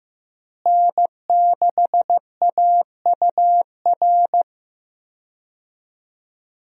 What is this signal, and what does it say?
Morse code, heard as a keyed tone, about 16 wpm, 700 Hz. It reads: N6AUR
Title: USA Call Signs - 5 Characters - Repeated 3x - 15wpm